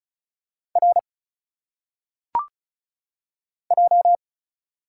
Morse code, heard as a keyed tone, about 34 wpm, 700 Hz. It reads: R J
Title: Single Letters - Phonetic Pronunciation 35wpm